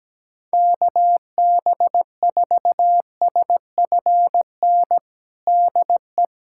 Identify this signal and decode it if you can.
Morse code, heard as a keyed tone, about 17 wpm, 700 Hz. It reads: KB4SFN DE
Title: USA Callsign DE Callsign - 6 Characters - 17wpm